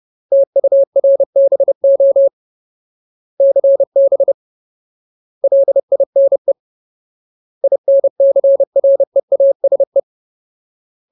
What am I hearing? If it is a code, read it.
Morse code, heard as a keyed tone, about 30 wpm, 550 Hz. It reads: TURBO CB LINE INCREASE
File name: Ultimate Word List / 30WPM Ultimate